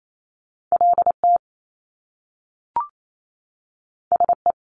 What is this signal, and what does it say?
Morse code, heard as a keyed tone, about 28 wpm, 700 Hz. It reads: LT SE
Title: Top Two Letter Combinations in Words 28wpm